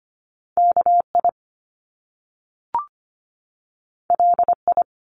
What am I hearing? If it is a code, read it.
Morse code, heard as a keyed tone, about 25 wpm, 700 Hz. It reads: KI LI
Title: Top Two Letter Combinations in Words 25wpm